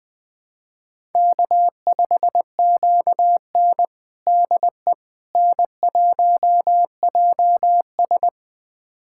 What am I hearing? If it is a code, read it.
Morse code, heard as a keyed tone, about 20 wpm, 700 Hz. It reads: K5QN DE N1JS
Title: USA Callsign DE Callsign - 4 Characters - 20wpm